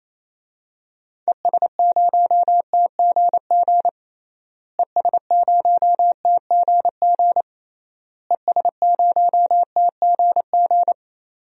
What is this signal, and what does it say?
Morse code, heard as a keyed tone, about 28 wpm, 700 Hz. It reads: ES0TGG ES0TGG ES0TGG
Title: DXCC-100 Call Signs - 6 Characters - Repeated 3x - 28wpm